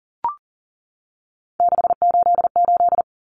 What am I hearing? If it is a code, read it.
Morse code, heard as a keyed tone, about 40 wpm, 700 Hz. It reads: 688
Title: Numbers - 3-Digit 40wpm